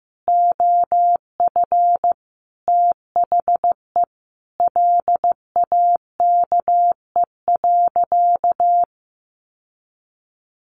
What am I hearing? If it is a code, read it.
Morse code, heard as a keyed tone, about 15 wpm, 700 Hz. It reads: OF THE LAKE.
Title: Sentences from Top 1400 Words 15wpm